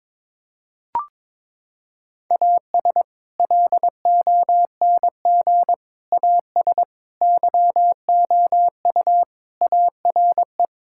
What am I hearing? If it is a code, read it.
Morse code, heard as a keyed tone, about 22 wpm, 700 Hz. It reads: AS LONG AS YOU ARE
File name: practice / 22wpm / Sentences with 4-Word Phrases - 22wpm